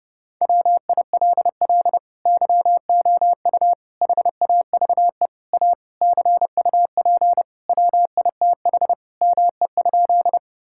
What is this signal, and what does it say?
Morse code, heard as a keyed tone, about 30 wpm, 700 Hz. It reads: WILL YOU HAVE A CUP WITH ME?
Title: Sentences from Top 1600 Words 30wpm